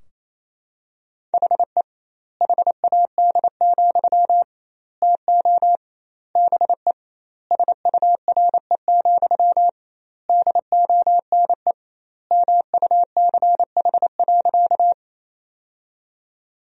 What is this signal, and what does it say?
Morse code, heard as a keyed tone, about 28 wpm, 700 Hz. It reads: HE HAD, TO BE SURE, DONE MUCH.
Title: 2x Word-Spacing - Sentences from Top 500 Words - 28wpm